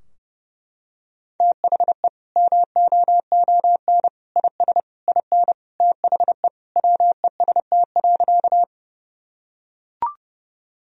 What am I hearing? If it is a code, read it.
Morse code, heard as a keyed tone, about 30 wpm, 700 Hz. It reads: THE MOON IS IN THE WEST.
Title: Sentences from Top 1000 Words - Review Part 3 - 30wpm